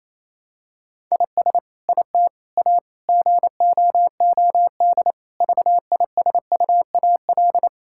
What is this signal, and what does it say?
Morse code, heard as a keyed tone, about 28 wpm, 700 Hz. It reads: IS IT A GOOD VISUAL
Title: Sentences from Top 1000 Words - Review Part 6 - 28wpm